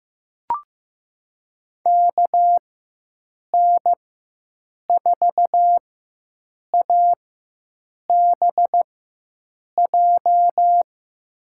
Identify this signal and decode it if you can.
Morse code, heard as a keyed tone, about 15 wpm, spaced as for 7 wpm, 700 Hz. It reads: KN4ABJ
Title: USA Call Signs - 6 Characters - 4x Character Spacing - 15wpm